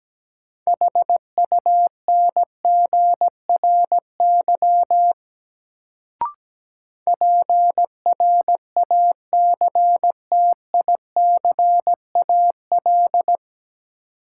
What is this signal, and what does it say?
Morse code, heard as a keyed tone, about 17 wpm, 700 Hz. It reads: HUNGRY PRACTICAL
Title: Sentences from Top 1400 Words - New Word Drill Alt - 17wpm